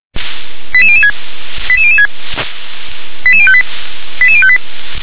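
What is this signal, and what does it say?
Five-tone selective call (ZVEI), heard as quick runs of five tones, tones of about 70 ms. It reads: 88006 88006 88058 88058
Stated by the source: ZVEI five-tone call 88085 received by SDR.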